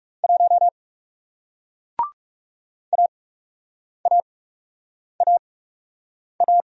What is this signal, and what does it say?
Morse code, heard as a keyed tone, about 42 wpm, spaced as for 8 wpm, 700 Hz. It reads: 1 AAAA